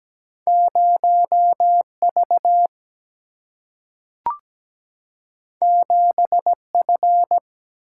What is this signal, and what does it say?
Morse code, heard as a keyed tone, about 17 wpm, 700 Hz. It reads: 0 V 7 F
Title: Single-Letter Single-Number 17wpm